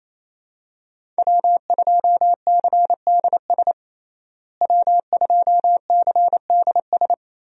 Morse code, heard as a keyed tone, about 27 wpm, 700 Hz. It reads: W2CDS W2CDS
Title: USA Call Signs - 5 Characters - Repeated 3x - 28wpm